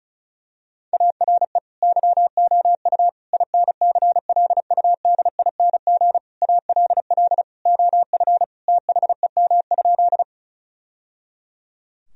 Morse code, heard as a keyed tone, about 36 wpm, 700 Hz. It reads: ARE YOU INCLUDING ALL OF THEM?
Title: Sentences from Top 1000 Words - Review Part 4 - 35wpm